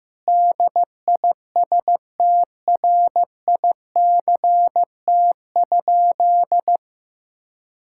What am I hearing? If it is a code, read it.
Morse code, heard as a keyed tone, about 15 wpm, 700 Hz. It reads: DISTRICT?
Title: Sentences from Top 1600 Words 15wpm